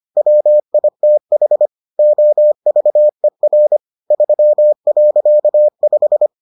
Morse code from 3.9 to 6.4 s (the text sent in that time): 3.5